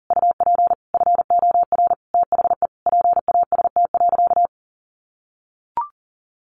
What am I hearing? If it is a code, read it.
Morse code, heard as a keyed tone, about 40 wpm, 700 Hz. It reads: UP FOR THE PAST.
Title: Sentences from Top 1100 Words 40wpm